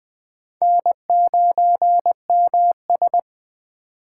Morse code, heard as a keyed tone, about 20 wpm, 700 Hz. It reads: N9MS
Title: K1USN SST QSO Practice - 20wpm